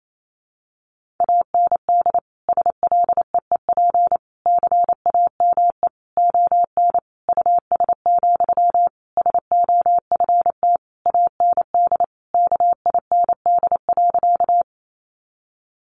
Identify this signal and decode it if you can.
Morse code, heard as a keyed tone, about 28 wpm, 700 Hz. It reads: AND SLEEP CAME ON US, SOFT AND KIND.
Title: Sentences from Top 1500 Words 28wpm